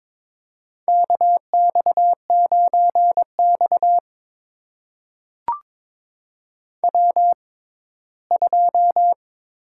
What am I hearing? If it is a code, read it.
Morse code, heard as a keyed tone, about 22 wpm, spaced as for 15 wpm, 700 Hz. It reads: KX9X W2
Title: USA Call Signs - 6x Character Spacing - 22wpm